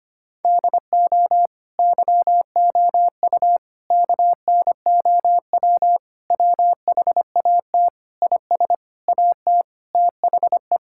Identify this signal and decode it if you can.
Morse code, heard as a keyed tone, about 25 wpm, 700 Hz. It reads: DO YOU KNOW WHAT IS AT THE
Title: Sentences from Top 1000 Words - Review Part 6 - 25wpm